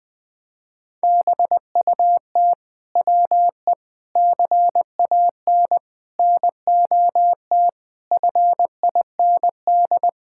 Morse code, heard as a keyed tone, about 20 wpm, 700 Hz. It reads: BUT WE CAN NOT FIND
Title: Sentences from Top 1000 Words - Review Part 5 - 20wpm